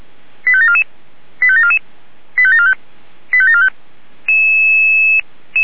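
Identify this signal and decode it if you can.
Five-tone selective call (ZVEI), heard as quick runs of five tones, tones of about 70 ms. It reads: 85744 85744 85745 85745
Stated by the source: ZVEI five-tone call received by SDR, from the Feuerwehr Orlamünde audio archive.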